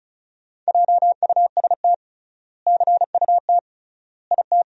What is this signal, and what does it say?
Morse code, heard as a keyed tone, about 36 wpm, 700 Hz. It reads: JUST CUT IT
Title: 3x Word-Spacing - Sentences from Top 300 Words - 35wpm